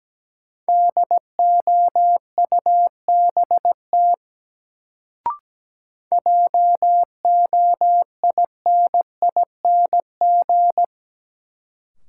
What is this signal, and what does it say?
Morse code, heard as a keyed tone, about 17 wpm, 700 Hz. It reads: DOUBT JOINING
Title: Sets of 5 Words - New Word Drill - 17wpm